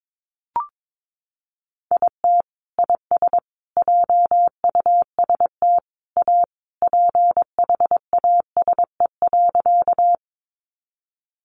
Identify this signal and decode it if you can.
Morse code, heard as a keyed tone, about 22 wpm, 700 Hz. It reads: IT IS JUST A PHASE.
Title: Sentences from Top 1200 Words 22wpm